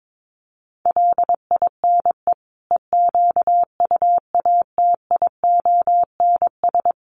Morse code, heard as a keyed tone, about 22 wpm, 700 Hz. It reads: LINE EQUATIONS
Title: Sets of 5 Words 22wpm